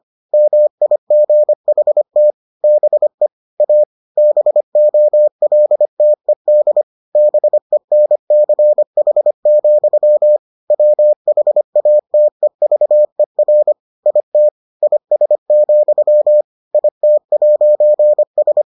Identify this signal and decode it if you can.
Morse code, heard as a keyed tone, about 25 wpm, 600 Hz. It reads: MIGHT BE A BOLTED BENCH, WHATEVER IT IS, IT'S